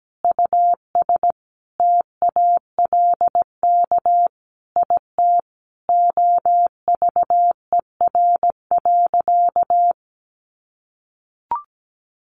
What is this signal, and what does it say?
Morse code, heard as a keyed tone, about 17 wpm, 700 Hz. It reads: US TALK IT OVER.
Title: Sentences from Top 500 Words 17wpm